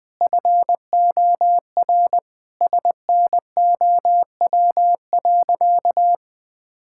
Morse code, heard as a keyed tone, about 20 wpm, 700 Hz. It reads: FOR SNOW.
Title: Sentences with 4-Word Phrases - 20wpm